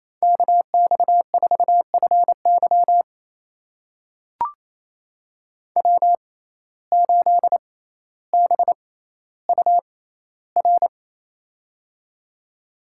Morse code, heard as a keyed tone, about 28 wpm, spaced as for 14 wpm, 700 Hz. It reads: KX4FY W8BUR
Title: USA Call Signs - 5 Characters - 6x Character Spacing - 28wpm